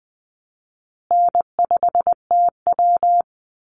Morse code, heard as a keyed tone, about 20 wpm, 700 Hz. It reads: N5TW
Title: Easy Call Signs Repeated 3x - 20wpm